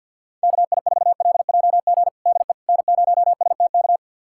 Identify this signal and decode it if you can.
Morse code, heard as a keyed tone, about 51 wpm, 700 Hz. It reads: KI4LJG DE N0STK
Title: USA Callsign DE Callsign 50wpm